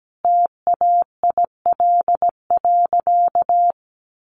Morse code, heard as a keyed tone, about 17 wpm, 700 Hz. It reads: TAIL.
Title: Easy Phrases 17wpm